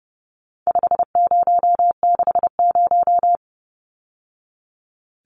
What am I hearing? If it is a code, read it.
Morse code, heard as a keyed tone, about 30 wpm, 700 Hz. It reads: 5060
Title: Numbers - 4-Digit 30wpm